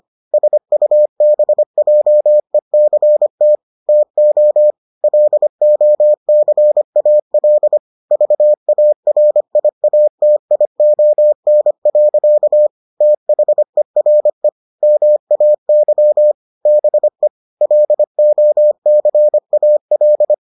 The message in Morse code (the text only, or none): SUBJECT TO LOCAL VARIATION. THERE MAY BE LOCAL